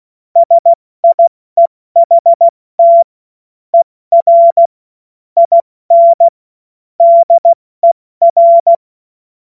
Morse code, heard as a keyed tone, about 16 wpm, spaced as for 14 wpm, 680 Hz. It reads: SIEHT ER IN DER
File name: Witz-16-September-2025-70-F80